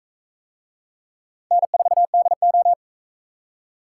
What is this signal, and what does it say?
Morse code, heard as a keyed tone, about 41 wpm, 700 Hz. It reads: N4DO